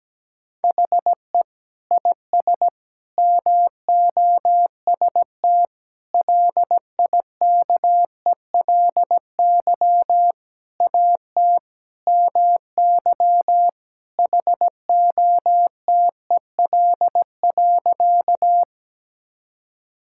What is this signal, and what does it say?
Morse code, heard as a keyed tone, about 17 wpm, 700 Hz. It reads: HE IS MOST LIKELY AT MY HOTEL.
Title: Sentences from Top 1600 Words 17wpm